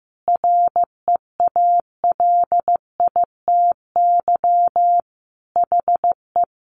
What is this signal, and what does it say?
Morse code, heard as a keyed tone, about 15 wpm, 700 Hz. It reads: REALITY HE
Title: Sentences from Top 1200 Words 15wpm